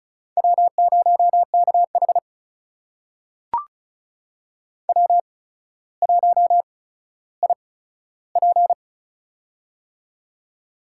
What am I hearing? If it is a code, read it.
Morse code, heard as a keyed tone, about 36 wpm, spaced as for 15 wpm, 700 Hz. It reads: W0KH W1IP
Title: USA Call Signs - 4 Characters - 8x Character Spacing - 35wpm